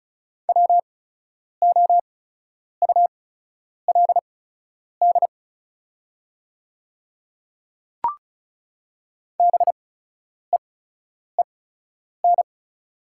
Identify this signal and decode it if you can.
Morse code, heard as a keyed tone, about 36 wpm, spaced as for 10 wpm, 690 Hz. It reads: WOULD BEEN
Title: 8x Character Spacing - Top 100 Words - 35wpm